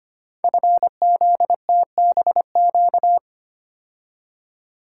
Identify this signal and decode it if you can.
Morse code, heard as a keyed tone, about 25 wpm, 700 Hz. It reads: FZTBQ
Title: Code Groups - 5 Letters - 25wpm